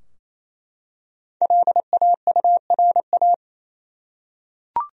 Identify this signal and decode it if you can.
Morse code, heard as a keyed tone, about 28 wpm, 700 Hz. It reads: LAURA
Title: USA Common Names - Max 5 Letters - 28wpm